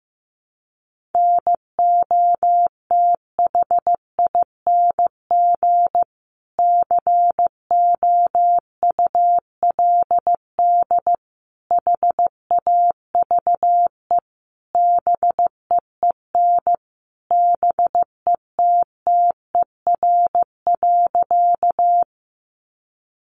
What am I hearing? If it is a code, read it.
Morse code, heard as a keyed tone, about 15 wpm, 700 Hz. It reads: NOTHING COULD HAVE BEEN BETTER.
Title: Sentences from Top 500 Words 15wpm